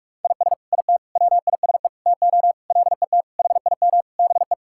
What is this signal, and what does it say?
Morse code, heard as a keyed tone, about 46 wpm, 700 Hz. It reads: IS IT WISE TO LET HIM BE
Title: Sentences from Top 1600 Words 45wpm